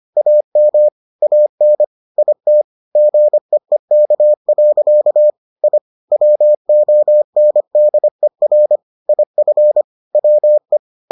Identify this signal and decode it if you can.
Morse code, heard as a keyed tone, about 25 wpm, 600 Hz. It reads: AM AN IT GEEK. I WONDER IF WE